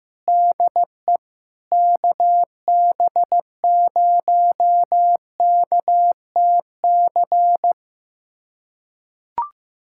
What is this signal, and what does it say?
Morse code, heard as a keyed tone, about 15 wpm, 700 Hz. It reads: DE KB0KTC
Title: USA Callsign DE Callsign 15wpm